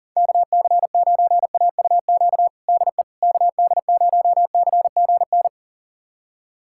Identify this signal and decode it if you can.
Morse code, heard as a keyed tone, about 40 wpm, 700 Hz. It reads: KC9AUQ DE KD0CGN
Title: USA Callsign DE Callsign - 6 Characters - 40wpm